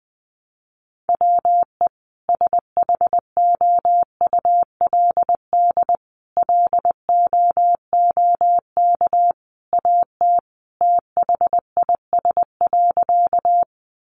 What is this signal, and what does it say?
Morse code, heard as a keyed tone, about 20 wpm, 700 Hz. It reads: WE SHOULD LOOK AT THIS.
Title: Sentences from Top 300 Words 20wpm